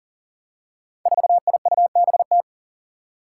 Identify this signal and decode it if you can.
Morse code, heard as a keyed tone, about 40 wpm, 700 Hz. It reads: 4IUBT
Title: Five Challenging Letter-Number Combinations 40wpm